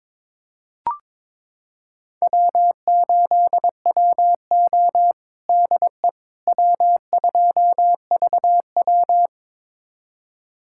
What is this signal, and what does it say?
Morse code, heard as a keyed tone, about 22 wpm, 700 Hz. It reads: W8WO DE W2VW
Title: USA Callsign DE Callsign - 4 Characters - 22wpm